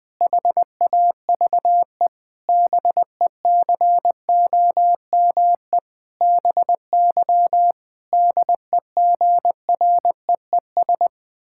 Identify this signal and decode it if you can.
Morse code, heard as a keyed tone, about 19 wpm, 700 Hz. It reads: HAVE BECOME BY DEGREES
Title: Pangram Challenge v2 - 20wpm